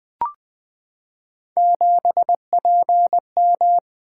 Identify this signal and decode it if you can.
Morse code, heard as a keyed tone, about 20 wpm, 700 Hz. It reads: 7PM
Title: Three Letter-Number Combinations - 20wpm